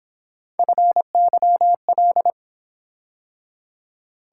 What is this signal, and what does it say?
Morse code, heard as a keyed tone, about 26 wpm, 700 Hz. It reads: FYL
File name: Speed-Racing - Three Letter-Number Combinations - 22wpm